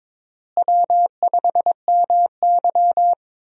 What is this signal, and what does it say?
Morse code, heard as a keyed tone, about 21 wpm, 700 Hz. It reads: W5MY
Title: USA Call Signs - 4 Characters - Repeated 3x - 22wpm